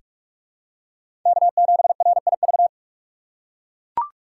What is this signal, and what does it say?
Morse code, heard as a keyed tone, about 43 wpm, 700 Hz. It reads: K7RIV
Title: USA Call Signs - 5 Characters - 3x Character Spacing - 45wpm